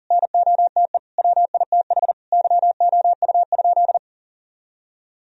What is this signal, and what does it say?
Morse code, heard as a keyed tone, about 37 wpm, 700 Hz. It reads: NOTE WITH YOU?
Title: Sentences from Top 1000 Words - Review Part 2 - 40wpm